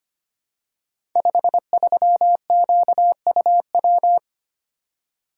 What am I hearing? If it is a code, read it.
Morse code, heard as a keyed tone, about 25 wpm, 700 Hz. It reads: 53QUW
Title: Code Groups - 5 Letter:Numbers - 25wpm